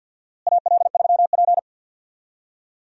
Morse code, heard as a keyed tone, about 51 wpm, 700 Hz. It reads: AL3P
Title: USA Call Signs - 4 Characters - Repeated 3x - 50wpm